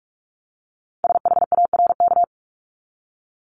Morse code, heard as a keyed tone, about 46 wpm, 700 Hz. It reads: SHARK
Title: Multi-letter Phonemes 45wpm